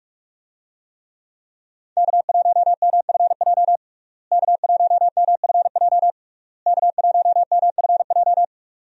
Morse code, heard as a keyed tone, about 41 wpm, 700 Hz. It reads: K1MFJ K1MFJ K1MFJ
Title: USA Call Signs - 5 Characters - Repeated 3x - 45wpm